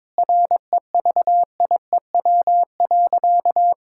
Morse code, heard as a keyed tone, about 22 wpm, 700 Hz. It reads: REVIEW.
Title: Sentences from Top 1000 Words - Review Part 6 - 22wpm